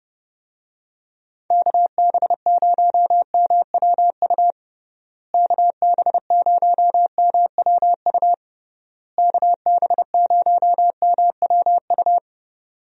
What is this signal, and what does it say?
Morse code, heard as a keyed tone, about 29 wpm, 700 Hz. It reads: KB0MWU KB0MWU KB0MWU
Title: USA Call Signs - 6 Characters - Repeated 3x - 30wpm